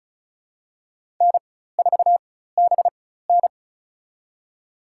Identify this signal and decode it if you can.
Morse code, heard as a keyed tone, about 34 wpm, spaced as for 16 wpm, 700 Hz. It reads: N4BN
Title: USA Call Signs - 4 Characters - 4x Character Spacing - 35wpm